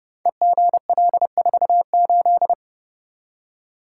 Morse code, heard as a keyed tone, about 28 wpm, 700 Hz. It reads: EGL48
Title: Code Groups - 5 Letter:Numbers - 30wpm